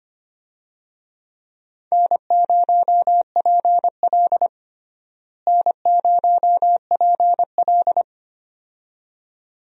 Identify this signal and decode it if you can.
Morse code, heard as a keyed tone, about 25 wpm, 700 Hz. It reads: N0PL N0PL
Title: USA Call Signs - 4 Characters - Repeated 2x - 25wpm